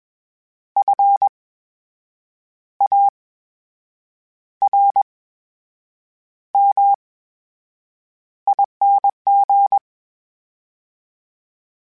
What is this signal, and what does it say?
Morse code, heard as a keyed tone, about 21 wpm, spaced as for 7 wpm, 800 Hz. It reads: FARMING
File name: Special INGs-mp3s / Special INGs-mp3s / VerySlowToFaster Ing @20